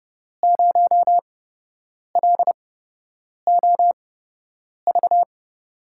Morse code, heard as a keyed tone, about 30 wpm, spaced as for 8 wpm, 700 Hz. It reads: 0LOV
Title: USA Call Signs - 8x Character Spacing - 30wpm